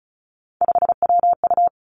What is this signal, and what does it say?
Morse code, heard as a keyed tone, about 36 wpm, 700 Hz. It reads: 5WU
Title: Three Challenging Letter-Number Combinations 35wpm